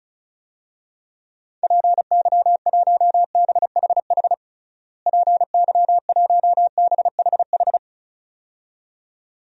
Morse code, heard as a keyed tone, about 36 wpm, 700 Hz. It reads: PY1BHH PY1BHH
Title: DXCC-100 Call Signs - 6 Characters - Repeated 2x - 35wpm